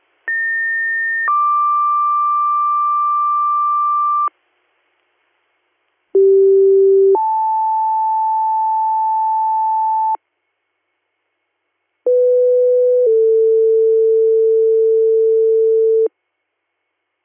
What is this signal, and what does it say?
Two-tone page, heard as a one-second tone, then a three-second tone, about 1800 then 1180 Hz, about 390 then 880 Hz, about 500 then 450 Hz. It is heard from 0.3 to 16.1 s.